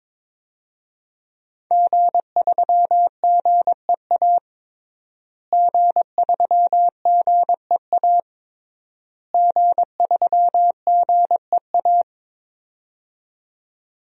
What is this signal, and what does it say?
Morse code, heard as a keyed tone, about 22 wpm, 700 Hz. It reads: G3GEA G3GEA G3GEA
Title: DXCC-100 Call Signs - 5 Characters - Repeated 3x - 22wpm